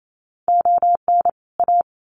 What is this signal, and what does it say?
Morse code, heard as a keyed tone, about 28 wpm, 700 Hz. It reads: ON A
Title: Sets of 2 Words 28wpm